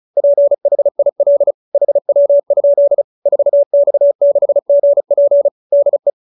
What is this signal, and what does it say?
Morse code, heard as a keyed tone, about 34 wpm, 590 Hz. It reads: PHIL HW? 4X6GP DE